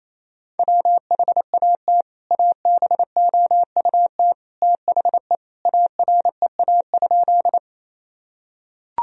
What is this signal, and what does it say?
Morse code, heard as a keyed tone, about 28 wpm, 700 Hz. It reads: WHAT ABOUT THE AREA?
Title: Sentences from Top 1000 Words - Review Part 2 - 28wpm